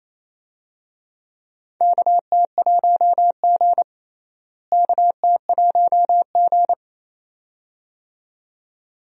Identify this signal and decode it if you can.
Morse code, heard as a keyed tone, about 28 wpm, 700 Hz. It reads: KT1G KT1G
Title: USA Call Signs - 4 Characters - Repeated 2x - 28wpm